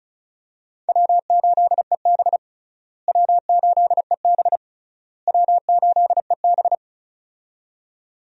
Morse code, heard as a keyed tone, about 36 wpm, 700 Hz. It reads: W8EB W8EB W8EB
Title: USA Call Signs - 4 Characters - Repeated 3x - 35wpm